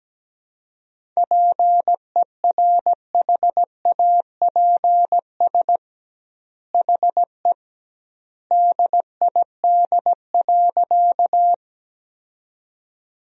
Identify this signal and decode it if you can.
Morse code, heard as a keyed tone, about 17 wpm, 700 Hz. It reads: PERHAPS HE DID.
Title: 2x Word-Spacing - Sentences from Top 500 Words - 17wpm